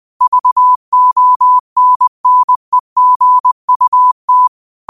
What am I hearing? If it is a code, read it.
Morse code, heard as a keyed tone, about 20 wpm, 1000 Hz. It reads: VONNEGUT